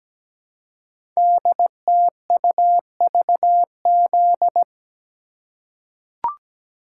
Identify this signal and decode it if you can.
Morse code, heard as a keyed tone, about 17 wpm, 700 Hz. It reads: DTUVZ
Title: Code Groups - 5 Letters - 17wpm